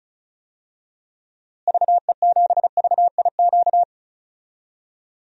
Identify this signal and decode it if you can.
Morse code, heard as a keyed tone, about 36 wpm, 700 Hz. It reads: VE7VIQ
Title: DXCC-100 Call Signs - 6 Characters - Repeated 2x - 35wpm